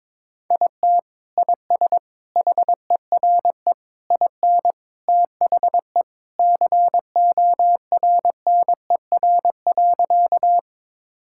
Morse code, heard as a keyed tone, about 22 wpm, 700 Hz. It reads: IT IS HERE IN THE CORNER.
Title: Sentences from Top 1600 Words 22wpm